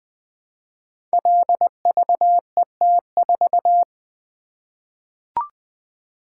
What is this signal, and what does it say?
Morse code, heard as a keyed tone, about 20 wpm, 700 Hz. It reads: LVET4
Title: Code Groups - 5 Letter:Numbers - 20wpm